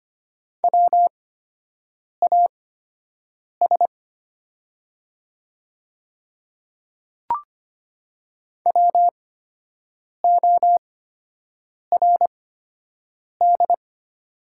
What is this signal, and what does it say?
Morse code, heard as a keyed tone, about 25 wpm, spaced as for 7 wpm, 700 Hz. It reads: WAS WORD
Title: 8x Character Spacing - Top 100 Words - 25wpm